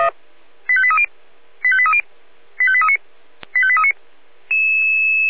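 Five-tone selective call (ZVEI), heard as quick runs of five tones, tones of about 70 ms. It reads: 85830 85830 85839 85839
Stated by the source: ZVEI five-tone selective call, received on an SDR.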